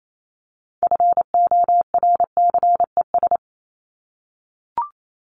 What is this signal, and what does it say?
Morse code, heard as a keyed tone, about 28 wpm, 700 Hz. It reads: FORCES
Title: ES:S Suffix - Top 100 Words with Base Words 28wpm